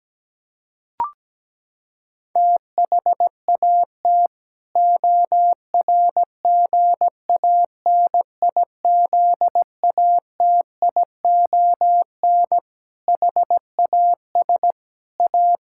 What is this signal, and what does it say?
Morse code, heard as a keyed tone, about 17 wpm, 700 Hz. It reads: THAT ORGANIZATION HAS A
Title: Sentences from Top 1000 Words - Review Part 6 - 17wpm